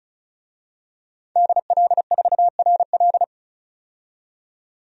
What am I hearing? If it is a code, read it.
Morse code, heard as a keyed tone, about 34 wpm, 700 Hz. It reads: DL4RL
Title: DXCC-100 Call Signs - 5 Characters - 35wpm